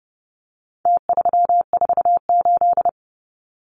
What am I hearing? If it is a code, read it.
Morse code, heard as a keyed tone, about 30 wpm, 700 Hz. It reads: T348
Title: Numbers - Contest Style - 4-Digit 30wpm